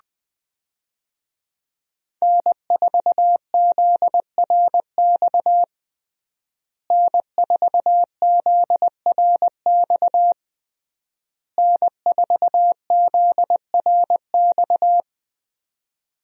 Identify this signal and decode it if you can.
Morse code, heard as a keyed tone, about 20 wpm, 700 Hz. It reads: N4ZRX N4ZRX N4ZRX
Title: USA Call Signs - 5 Characters - Repeated 3x - 20wpm